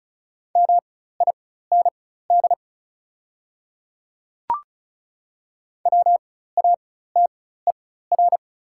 Morse code, heard as a keyed tone, about 34 wpm, spaced as for 16 wpm, 700 Hz. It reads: MIND WATER
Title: 4x Character Spacing - Top 500 Words - 35wpm